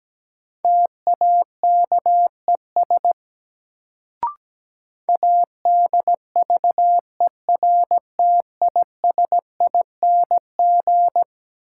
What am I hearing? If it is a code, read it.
Morse code, heard as a keyed tone, about 17 wpm, 700 Hz. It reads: TAKES ADVERTISING
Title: Sentences from Top 1000 Words - Review Part 6 - New Word Drill - 17wpm